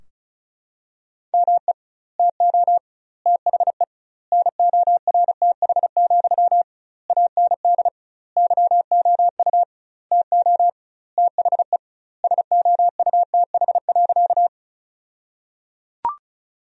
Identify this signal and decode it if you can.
Morse code, heard as a keyed tone, about 34 wpm, 700 Hz. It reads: ME TO THE NORTH, AND YOU TO THE SOUTH.
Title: 2x Word-Spacing - Sentences from Top 400 Words - 35wpm